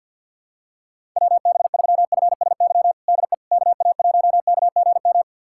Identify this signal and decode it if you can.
Morse code, heard as a keyed tone, about 52 wpm, 700 Hz. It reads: WB3FSY DE KA1KGM
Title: USA Callsign DE Callsign - 6 Characters - 50wpm